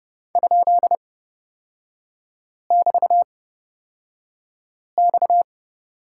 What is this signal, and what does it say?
Morse code, heard as a keyed tone, about 31 wpm, spaced as for 5 wpm, 700 Hz. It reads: ?<BT>X
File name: Lesson 37e - Letter x - Three Characters 30wpm